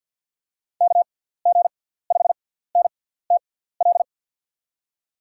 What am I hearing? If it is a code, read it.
Morse code, heard as a keyed tone, about 49 wpm, spaced as for 17 wpm, 700 Hz. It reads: KG5NTL
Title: USA Call Signs - 6 Characters - 6x Character Spacing - 50wpm